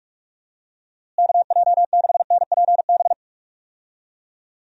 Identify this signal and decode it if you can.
Morse code, heard as a keyed tone, about 46 wpm, 700 Hz. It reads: KJ6NPB
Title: USA Call Signs - 6 Characters - Repeated 3x - 45wpm